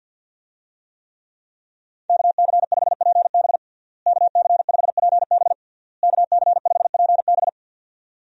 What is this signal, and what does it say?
Morse code, heard as a keyed tone, about 51 wpm, 700 Hz. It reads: KC5PB KC5PB KC5PB
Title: USA Call Signs - 5 Characters - Repeated 3x - 50wpm